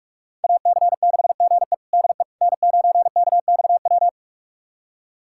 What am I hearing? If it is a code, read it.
Morse code, heard as a keyed tone, about 46 wpm, 700 Hz. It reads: AC6GE DE N9KXW
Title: USA Callsign DE Callsign - 5 Characters - 45wpm